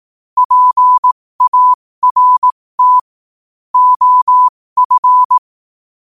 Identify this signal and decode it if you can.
Morse code, heard as a keyed tone, about 18 wpm, spaced as for 15 wpm, 1000 Hz. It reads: PART OF
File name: qotdfiles / 15wpm